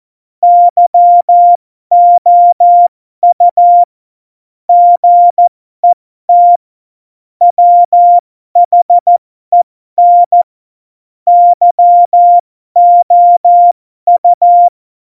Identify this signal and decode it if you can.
Morse code, heard as a keyed tone, about 14 wpm, spaced as for 12 wpm, 700 Hz. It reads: YOU GET WHEN YOU
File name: Morsekurs / Witze